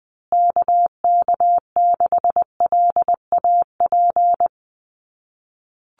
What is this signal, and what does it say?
Morse code, heard as a keyed tone, about 20 wpm, 700 Hz. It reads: KK6LAP
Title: USA Call Signs Repeated 2x 20wpm